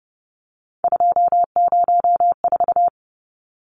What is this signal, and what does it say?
Morse code, heard as a keyed tone, about 30 wpm, 700 Hz. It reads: 204
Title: Numbers - 3-Digit 30wpm